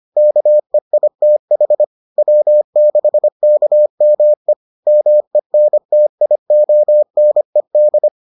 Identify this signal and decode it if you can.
Morse code, heard as a keyed tone, about 25 wpm, 600 Hz. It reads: KEITH W6KME MENTIONED